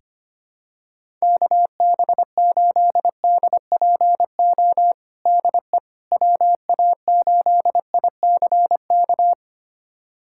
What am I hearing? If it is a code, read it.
Morse code, heard as a keyed tone, about 25 wpm, 700 Hz. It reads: KB8DPO DE WA8ICK
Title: USA Callsign DE Callsign - 6 Characters - 25wpm